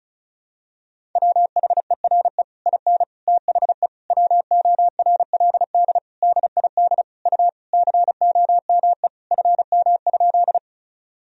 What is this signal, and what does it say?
Morse code, heard as a keyed tone, about 34 wpm, 690 Hz. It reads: WHERE IN THE WORLD DID U COME FM?
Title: Sentences from Top 200 Words and Abbreviations 35wpm